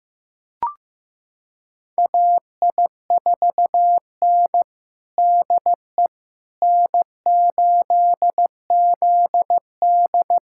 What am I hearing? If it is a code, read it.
Morse code, heard as a keyed tone, about 15 wpm, 700 Hz. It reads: AI4N DE N8ZD
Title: USA Callsign DE Callsign - 4 Characters - 15wpm